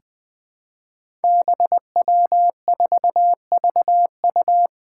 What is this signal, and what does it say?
Morse code, heard as a keyed tone, about 20 wpm, 700 Hz. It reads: BW4VU
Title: Five Challenging Letter-Number Combinations 20wpm